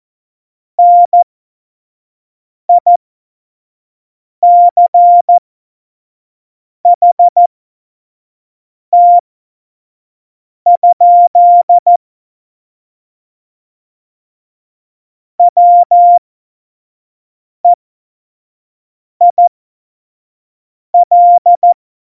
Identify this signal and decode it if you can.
Morse code, heard as a keyed tone, about 14 wpm, spaced as for 5 wpm, 700 Hz. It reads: NICHT? WEIL
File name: Morsekurs / Witze / Witz 28. Januar 2020, 25 F70